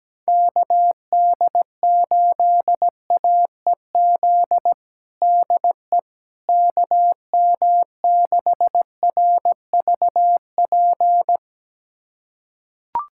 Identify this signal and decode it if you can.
Morse code, heard as a keyed tone, about 17 wpm, 700 Hz. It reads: KD8AEZ DE KM6RVP
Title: USA Callsign DE Callsign - 6 Characters - 17wpm